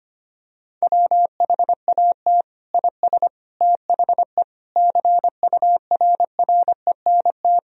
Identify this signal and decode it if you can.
Morse code, heard as a keyed tone, about 25 wpm, 700 Hz. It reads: WHAT IS THE CURRENT
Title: Sentences from Top 1000 Words - Review Part 4 - 25wpm